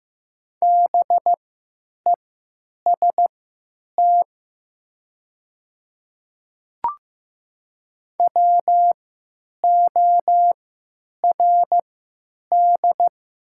Morse code, heard as a keyed tone, about 15 wpm, spaced as for 9 wpm, 700 Hz. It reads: BEST WORD
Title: 3x Character Spacing - Top 500 Words - 15wpm